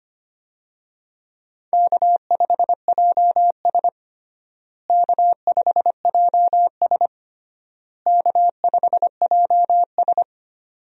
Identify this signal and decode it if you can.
Morse code, heard as a keyed tone, about 25 wpm, 700 Hz. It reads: K5JS K5JS K5JS
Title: USA Call Signs - 4 Characters - Repeated 3x - 25wpm